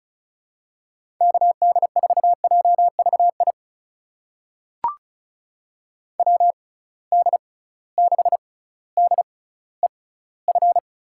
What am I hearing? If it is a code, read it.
Morse code, heard as a keyed tone, about 36 wpm, spaced as for 18 wpm, 700 Hz. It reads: KD4JVI WD6DEF
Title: USA Call Signs - 6 Characters - 6x Character Spacing - 35wpm